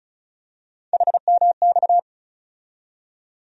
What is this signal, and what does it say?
Morse code, heard as a keyed tone, about 36 wpm, 700 Hz. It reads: HMX
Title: USA Call Sign Postfixes - 3 Letters 35wpm